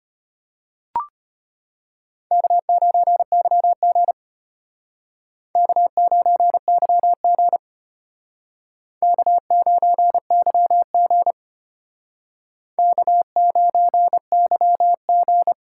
Morse code, heard as a keyed tone, about 32 wpm, 700 Hz. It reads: K9YG K9YG K9YG K9YG